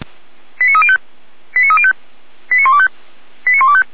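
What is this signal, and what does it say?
Five-tone selective call (ZVEI), heard as quick runs of five tones, tones of about 70 ms. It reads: 89305 89305 89125 89125
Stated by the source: ZVEI five-tone selective call received via SDR.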